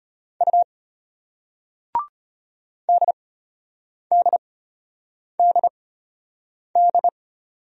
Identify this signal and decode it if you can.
Morse code, heard as a keyed tone, about 33 wpm, spaced as for 8 wpm, 700 Hz. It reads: U DDDD